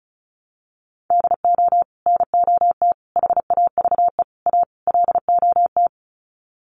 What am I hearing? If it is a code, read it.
Morse code, heard as a keyed tone, about 34 wpm, 700 Hz. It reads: DO NOT HAVE A LOT
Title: Sets of 5 Words Encore 35wpm